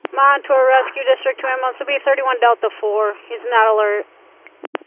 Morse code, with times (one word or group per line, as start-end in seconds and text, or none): none